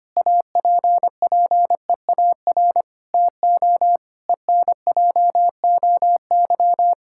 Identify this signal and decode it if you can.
Morse code, heard as a keyed tone, about 25 wpm, 700 Hz. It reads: APPEAR TO ENJOY
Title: Sentences from Top 1000 Words - Review Part 5 - 25wpm